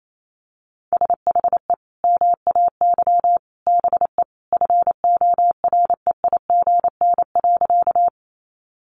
Morse code, heard as a keyed tone, about 28 wpm, 700 Hz. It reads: SHE MAY BE FOREIGN.
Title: Sentences from Top 1400 Words 28wpm